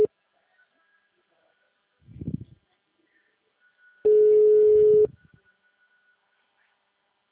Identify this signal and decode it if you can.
Ringback tone (Europe, 425 Hz).
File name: out- -2007- - - .14